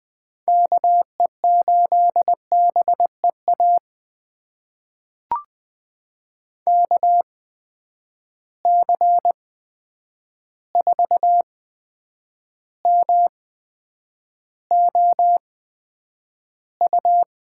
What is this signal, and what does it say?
Morse code, heard as a keyed tone, about 20 wpm, spaced as for 9 wpm, 700 Hz. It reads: KE8BEA KC4MOU
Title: USA Call Signs - 6 Characters - 8x Character Spacing - 20wpm